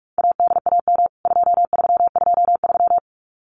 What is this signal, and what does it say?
Morse code, heard as a keyed tone, about 44 wpm, 700 Hz. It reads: ADAM 2323
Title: CWT Exchange 45wpm